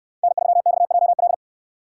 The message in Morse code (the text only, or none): N3XJL